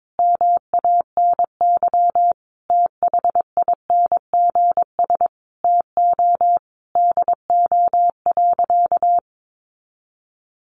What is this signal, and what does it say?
Morse code, heard as a keyed tone, about 22 wpm, 700 Hz. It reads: MANY THINGS TO DO.
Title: Additional Easy Phrases 22wpm